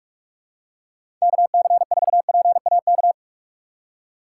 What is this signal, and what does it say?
Morse code, heard as a keyed tone, about 43 wpm, 700 Hz. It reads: KC4PAK
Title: USA Call Signs - 6 Characters - 2x Character Spacing - 45wpm